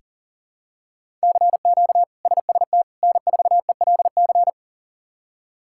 Morse code, heard as a keyed tone, about 37 wpm, 700 Hz. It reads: CQ SST N4ELC
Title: K1USN SST QSO Practice - 40wpm